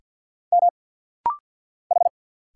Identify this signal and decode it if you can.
Morse code, heard as a keyed tone, about 50 wpm, 700 Hz. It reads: M H
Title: Single Letter-Number - Rapid-Fire 50wpm